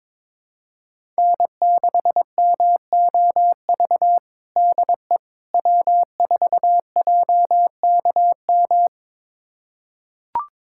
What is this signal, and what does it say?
Morse code, heard as a keyed tone, about 22 wpm, 700 Hz. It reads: N6MOV DE W4JKM
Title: USA Callsign DE Callsign - 5 Characters - 22wpm